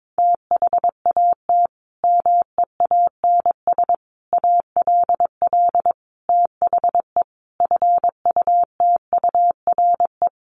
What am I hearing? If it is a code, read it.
Morse code, heard as a keyed tone, about 22 wpm, 700 Hz. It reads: THAT MEANS ALL THE FUTURE
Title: A Sound Of Thunder 22wpm